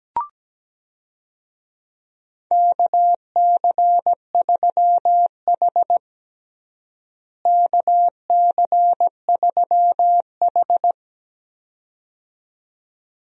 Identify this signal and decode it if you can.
Morse code, heard as a keyed tone, about 17 wpm, 700 Hz. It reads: KC3H KC3H
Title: USA Call Signs - 4 Characters - Repeated 2x - 17wpm